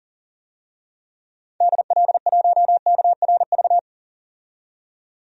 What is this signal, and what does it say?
Morse code, heard as a keyed tone, about 40 wpm, 700 Hz. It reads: DL1KRV
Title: DXCC-100 Call Signs - 6 Characters - 40wpm